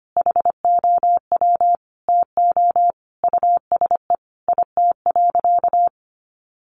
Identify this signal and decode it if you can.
Morse code, heard as a keyed tone, about 25 wpm, 700 Hz. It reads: HOW TO USE IT.
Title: Sentences from Top 500 Words 25wpm